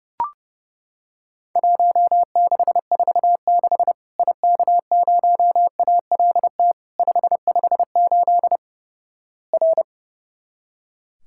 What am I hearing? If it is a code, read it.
Morse code, heard as a keyed tone, about 32 wpm, 700 Hz. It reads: 1646 IK0ALT 558 R
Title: Worked All Europe DX Contest - QTC Exchange - 30wpm